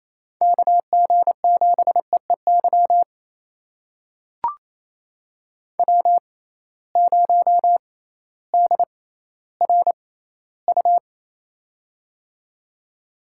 Morse code, heard as a keyed tone, about 27 wpm, spaced as for 15 wpm, 700 Hz. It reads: KG7EEY W0DRU
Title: USA Call Signs - 6x Character Spacing - 28wpm